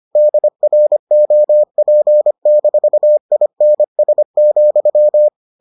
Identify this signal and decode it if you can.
Morse code, heard as a keyed tone, about 25 wpm, 600 Hz. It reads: DROP-INS,